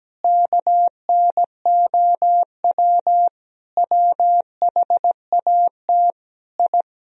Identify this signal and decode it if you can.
Morse code, heard as a keyed tone, about 17 wpm, 700 Hz. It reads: KNOW WHAT I
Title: Sentences from Top 1000 Words - Review Part 3 - 17wpm